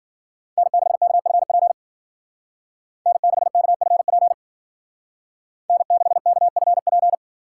Morse code, heard as a keyed tone, about 50 wpm, 700 Hz. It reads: N6KFP N6KFP N6KFP
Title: Speed Racing - USA Call Signs 40wpm